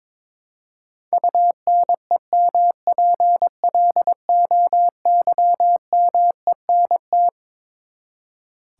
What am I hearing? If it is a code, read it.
Morse code, heard as a keyed tone, about 21 wpm, 700 Hz. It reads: UNEMPLOYMENT
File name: Twelve Letter Words - 22wpm